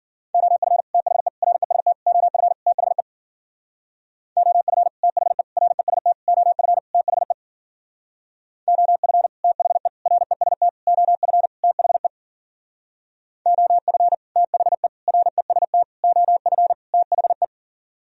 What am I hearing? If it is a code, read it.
Morse code, heard as a keyed tone, about 52 wpm, 700 Hz. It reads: OF THE REST OF THE OF THE REST OF THE OF THE REST OF THE OF THE REST OF THE